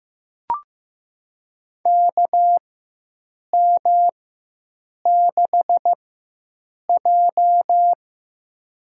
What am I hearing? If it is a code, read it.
Morse code, heard as a keyed tone, about 15 wpm, spaced as for 7 wpm, 700 Hz. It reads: KM6J
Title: USA Call Signs - 4 Characters - 4x Character Spacing - 15wpm